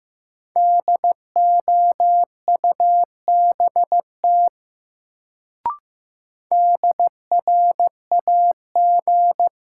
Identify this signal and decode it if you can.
Morse code, heard as a keyed tone, about 15 wpm, 700 Hz. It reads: DOUBT DRAG
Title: Sentences from Top 1500 Words - New Word Drill Alt - 15wpm